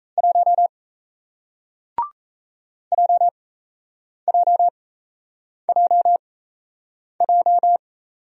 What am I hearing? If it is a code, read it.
Morse code, heard as a keyed tone, about 38 wpm, spaced as for 8 wpm, 700 Hz. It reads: 1 JJJJ